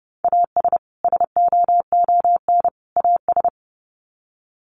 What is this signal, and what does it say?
Morse code, heard as a keyed tone, about 30 wpm, 700 Hz. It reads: AS SOON AS
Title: Sets of 3 Words 30wpm